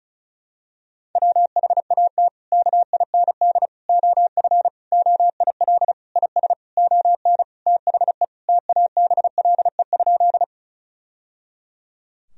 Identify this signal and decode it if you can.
Morse code, heard as a keyed tone, about 36 wpm, 700 Hz. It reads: WHAT KIND OF OIL IS ON THE TABLE?
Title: Sentences from Top 1000 Words - Review Part 4 - 35wpm